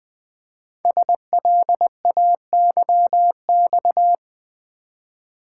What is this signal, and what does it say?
Morse code, heard as a keyed tone, about 20 wpm, 700 Hz. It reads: SLAYX
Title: Code Groups - 5 Letters - 20wpm